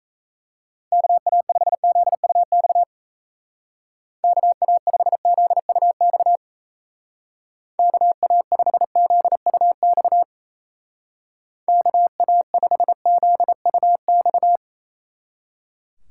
Morse code, heard as a keyed tone, about 35 wpm, 700 Hz. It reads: KA5ZUX KA5ZUX KA5ZUX KA5ZUX
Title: Speed Racing - USA 2x3 Call Signs 28wpm